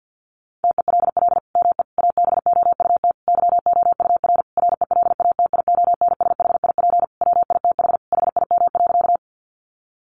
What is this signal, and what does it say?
Morse code, heard as a keyed tone, about 49 wpm, 700 Hz. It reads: TELL ME ABOUT YOUR RELATIONSHIP WITH HIM.
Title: Sentences from Top 1100 Words 50wpm